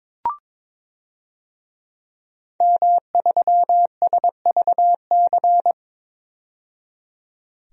Morse code, heard as a keyed tone, about 22 wpm, 700 Hz. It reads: M3SVC
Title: DXCC-100 Call Signs - 5 Characters - 22wpm